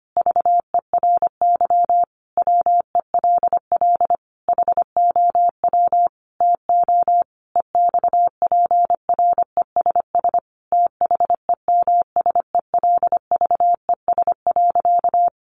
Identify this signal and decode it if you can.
Morse code, heard as a keyed tone, about 25 wpm, 700 Hz. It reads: VERY WELL HOW TO EXPRESS THEMSELVES.
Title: Sentences from Top 1100 Words 25wpm